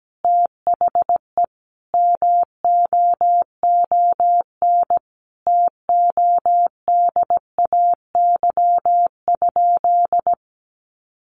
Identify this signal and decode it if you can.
Morse code, heard as a keyed tone, about 17 wpm, 700 Hz. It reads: THE MOON TODAY?
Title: Sentences from Top 1200 Words 17wpm